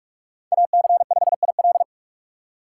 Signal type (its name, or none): Morse code